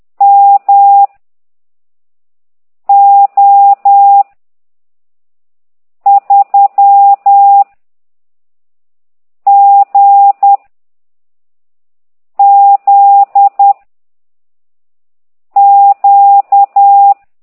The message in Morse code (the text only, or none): M O 3 G Z Q